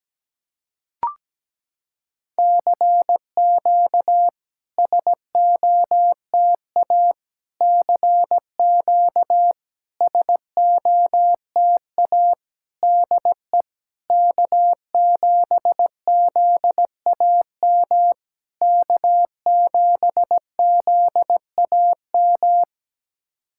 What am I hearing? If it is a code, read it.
Morse code, heard as a keyed tone, about 17 wpm, 700 Hz. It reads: CQ SOTA CQ SOTA DE K7ZAM K7ZAM
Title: SOTA QSO Practice 17wpm